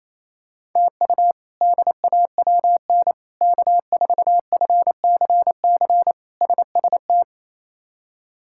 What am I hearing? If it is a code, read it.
Morse code, heard as a keyed tone, about 27 wpm, 700 Hz. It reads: TU DAWN K4FCC SST
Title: K1USN SST QSO Practice - 28wpm